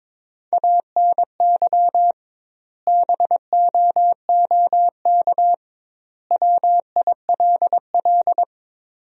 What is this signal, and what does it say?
Morse code, heard as a keyed tone, about 21 wpm, 700 Hz. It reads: ANY BOOK WILL
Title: 2x Word-Spacing - Sentences from Top 300 Words - 22wpm